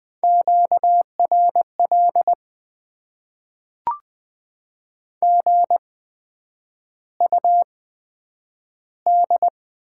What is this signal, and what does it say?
Morse code, heard as a keyed tone, about 20 wpm, spaced as for 9 wpm, 700 Hz. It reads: QRL GUD
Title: Top 20 QSO Elements - 8x Character Spacing - 20wpm